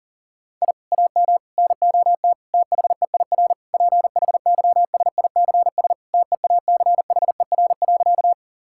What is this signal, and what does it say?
Morse code, heard as a keyed tone, about 37 wpm, 700 Hz. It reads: I AM NOT THEIR PHYSICS TEACHER.
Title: Sentences from Top 1000 Words - Review Part 6 - 40wpm